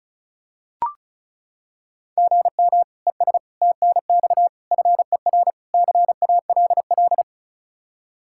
Morse code, heard as a keyed tone, about 34 wpm, 700 Hz. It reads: GM ES TNX FER CALL
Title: Practice QSO - Exchange 1 - Beginning - 35wpm